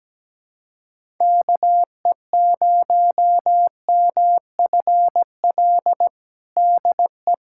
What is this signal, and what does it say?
Morse code, heard as a keyed tone, about 17 wpm, 700 Hz. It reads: KE0MFL DE
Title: USA Callsign DE Callsign - 6 Characters - 17wpm